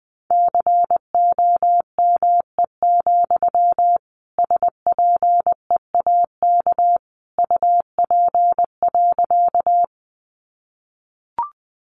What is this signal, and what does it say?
Morse code, heard as a keyed tone, about 20 wpm, 700 Hz. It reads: COME, SPEAK UP.
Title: Sentences from Top 1100 Words 20wpm